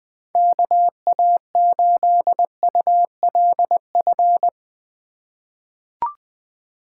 Morse code, heard as a keyed tone, about 20 wpm, 700 Hz. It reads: KA8ULF
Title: USA Call Signs - 8x Character Spacing - 20wpm